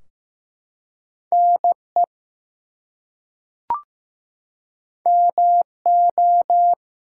Morse code, heard as a keyed tone, about 15 wpm, 700 Hz. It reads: NE MO
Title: US States and Canadian Provinces 15wpm